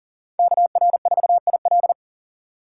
Morse code, heard as a keyed tone, about 40 wpm, 700 Hz. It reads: KR4IL
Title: USA Call Signs - 8x Character Spacing - 40wpm